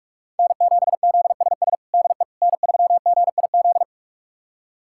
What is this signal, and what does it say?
Morse code, heard as a keyed tone, about 46 wpm, 700 Hz. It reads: N7ZSS DE N3GIZ
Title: USA Callsign DE Callsign - 5 Characters - 45wpm